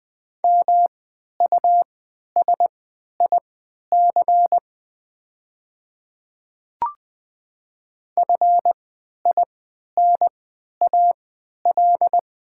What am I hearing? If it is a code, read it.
Morse code, heard as a keyed tone, about 20 wpm, spaced as for 11 wpm, 690 Hz. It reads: MUSIC FINAL
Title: 3x Character Spacing - Top 500 Words - 20wpm